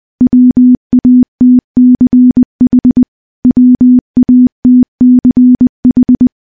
Morse code, heard as a keyed tone, about 20 wpm, 250 Hz. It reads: WATCH WATCH
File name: morse-training-20WPM